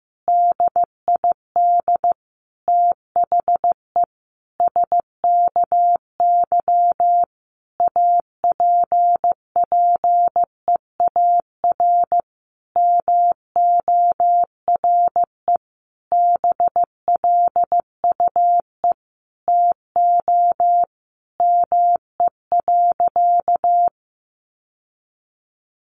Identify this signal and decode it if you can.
Morse code, heard as a keyed tone, about 15 wpm, 700 Hz. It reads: DID THE SKY APPEAR MORE BLUE TO ME.
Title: Sentences from Top 1200 Words 15wpm